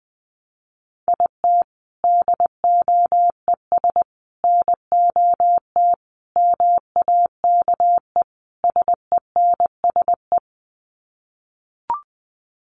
Morse code, heard as a keyed tone, about 19 wpm, 700 Hz. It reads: IT DOES NOT MAKE SENSE
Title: Sets of 5 Words Encore2 20wpm